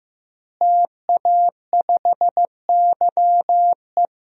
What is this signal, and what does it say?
Morse code, heard as a keyed tone, about 15 wpm, 700 Hz. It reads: TA5YE
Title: DXCC-100 Call Signs - 5 Characters - Repeated 3x - 15wpm